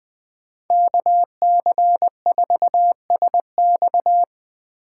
Morse code, heard as a keyed tone, about 24 wpm, 700 Hz. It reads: KC4SX
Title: Speed Racing - USA Call Signs 20wpm